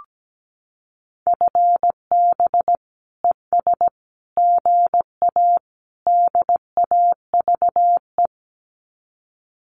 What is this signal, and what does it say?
Morse code, heard as a keyed tone, about 17 wpm, 700 Hz. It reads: FB ES GA DAVE
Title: Common QSO Phrases Encore 17wpm